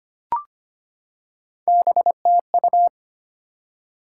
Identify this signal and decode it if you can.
Morse code, heard as a keyed tone, about 25 wpm, 700 Hz. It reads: BTU
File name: practice / 25wpm / Top 20 QSO Elements - 25wpm